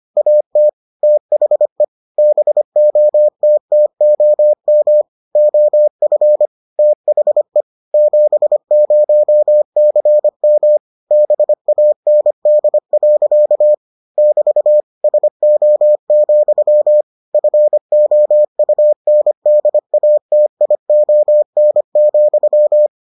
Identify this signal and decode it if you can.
Morse code, heard as a keyed tone, about 25 wpm, 600 Hz. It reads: AT THE BOTTOM OF THE 70CM BAND. <BT> SO, FOUNDATION,